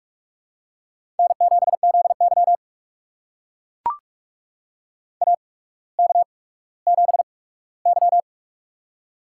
Morse code, heard as a keyed tone, about 44 wpm, spaced as for 19 wpm, 700 Hz. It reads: N7ZY AK7Y
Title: USA Call Signs - 4 Characters - 8x Character Spacing - 45wpm